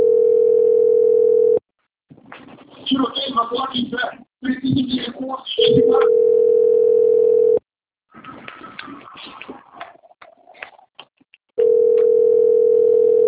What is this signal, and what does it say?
Ringback tone (North America), 440 + 480 Hz, with speech.
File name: out- - - - - .27363